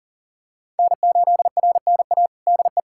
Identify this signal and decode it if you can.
Morse code, heard as a keyed tone, about 40 wpm, 700 Hz. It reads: N8RNA DE
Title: USA Callsign DE Callsign 40wpm